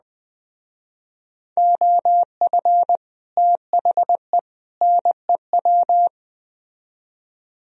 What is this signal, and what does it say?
Morse code, heard as a keyed tone, about 20 wpm, 700 Hz. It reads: OF THE NEW